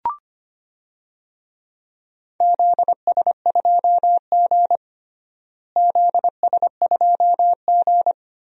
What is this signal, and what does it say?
Morse code, heard as a keyed tone, about 25 wpm, 700 Hz. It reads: ZS2G ZS2G
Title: DXCC-100 Call Signs - 4 Characters - Repeated 3x - 25wpm